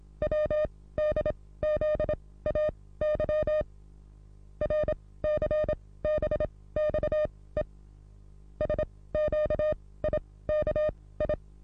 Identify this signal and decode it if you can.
Morse code, heard as a keyed tone, about 26 wpm, spaced as for 17 wpm, 610 Hz. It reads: WDZAY RCBXE SQIKI
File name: repos / Morsekurs / DJ4UF Morsen27